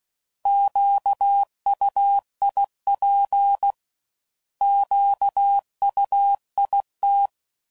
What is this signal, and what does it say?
Morse code, heard as a keyed tone, about 16 wpm, 800 Hz. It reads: QUIP QUIT
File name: Prefixes / By First Letter / Qs